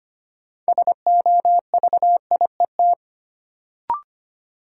Morse code, heard as a keyed tone, about 25 wpm, 700 Hz. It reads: SOVIET
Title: Sets of 5 Words Encore - New Word Drill - 25wpm